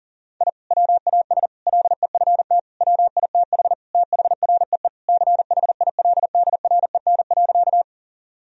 Morse code, heard as a keyed tone, about 40 wpm, 700 Hz. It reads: I WAS LEFT WITH THREE CHILDREN.
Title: Sentences from Top 1000 Words - Review Part 1 - 40wpm